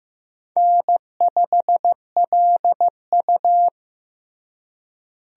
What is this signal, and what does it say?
Morse code, heard as a keyed tone, about 16 wpm, 700 Hz. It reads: N5LU
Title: USA Call Signs - 4 Characters - Repeated 3x - 15wpm